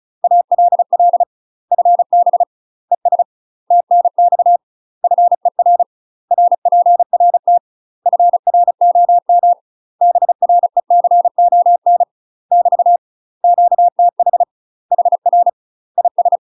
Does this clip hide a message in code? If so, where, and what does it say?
Yes: Morse code from 0.2 to 16.4 s, heard as a keyed tone, about 36 wpm, 700 Hz. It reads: ALL FB ES TNX FER RPRT FROM BRECON <BT> QTH HR IS